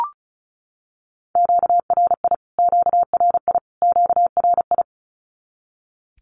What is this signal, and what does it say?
Morse code, heard as a keyed tone, about 34 wpm, 700 Hz. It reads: QRI QRI QRI
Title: Q-Codes Repeated 3x 35wpm